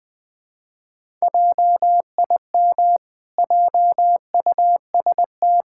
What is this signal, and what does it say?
Morse code, heard as a keyed tone, about 20 wpm, 700 Hz. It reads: JIM JUST
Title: Pangram Challenge v2 - 20wpm